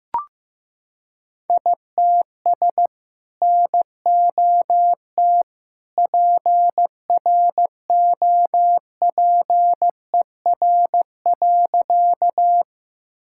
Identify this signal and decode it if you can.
Morse code, heard as a keyed tone, about 16 wpm, 700 Hz. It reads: ITS NOT PROPER.
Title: Sentences from Top 1000 Words - Review Part 5 - 15wpm